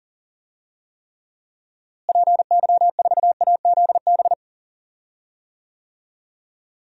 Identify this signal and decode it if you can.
Morse code, heard as a keyed tone, about 37 wpm, 700 Hz. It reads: PY4AZB
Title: DXCC-100 Call Signs - 6 Characters - Repeated 3x - 40wpm